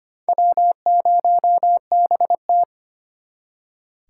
Morse code, heard as a keyed tone, about 32 wpm, 700 Hz. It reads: W0BT